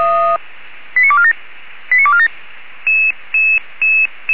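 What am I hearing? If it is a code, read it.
Five-tone selective call (ZVEI), heard as quick runs of five tones, tones of about 70 ms. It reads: 89247 89247
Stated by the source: Five-tone ZVEI selective call received by SDR.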